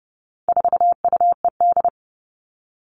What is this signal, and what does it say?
Morse code, heard as a keyed tone, about 30 wpm, 700 Hz. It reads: 4UED
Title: All Cut Numbers - 4-Digit 30wpm